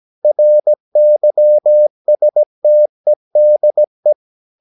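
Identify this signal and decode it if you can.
Morse code, heard as a keyed tone, about 17 wpm, 600 Hz. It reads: RYSTEDE
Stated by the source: Slow (QRS) Morse practice at 17 wpm, from CW med Gnister.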